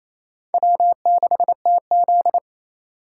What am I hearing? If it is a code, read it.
Morse code, heard as a keyed tone, about 35 wpm, 700 Hz. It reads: W6TZ